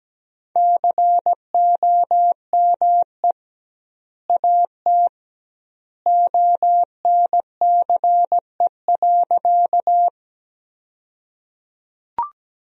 Morse code, heard as a keyed tone, about 17 wpm, 700 Hz. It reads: COME AT ONCE.
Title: 2x Word-Spacing - Sentences from Top 300 Words - 17wpm